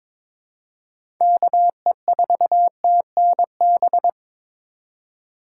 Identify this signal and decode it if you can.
Morse code, heard as a keyed tone, about 22 wpm, 700 Hz. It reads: KE4TNB
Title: USA Call Signs - 6 Characters - 2x Character Spacing - 22wpm